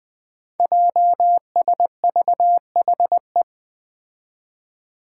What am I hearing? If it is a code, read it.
Morse code, heard as a keyed tone, about 20 wpm, 700 Hz. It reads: JSVHE
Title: Five Challenging Letter-Number Combinations 20wpm